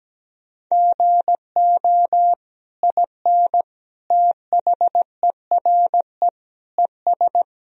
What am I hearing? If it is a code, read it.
Morse code, heard as a keyed tone, about 17 wpm, 700 Hz. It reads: GO IN THERE ES
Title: Sentences from Top 500 Words and Abbreviations 17wpm